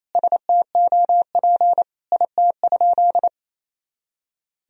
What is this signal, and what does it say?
Morse code, heard as a keyed tone, about 28 wpm, 700 Hz. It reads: STOP IT?
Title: Sentences from Top 300 Words and Abbreviations 28wpm